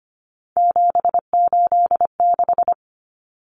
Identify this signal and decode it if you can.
Morse code, heard as a keyed tone, about 25 wpm, 700 Hz. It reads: 786
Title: Common Cut Numbers - 3-Digit 25wpm